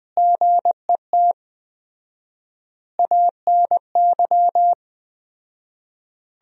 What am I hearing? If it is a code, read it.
Morse code, heard as a keyed tone, about 20 wpm, 700 Hz. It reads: GET ANY
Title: 4x Word-Spacing - Sentences from Top 400 Words - 20wpm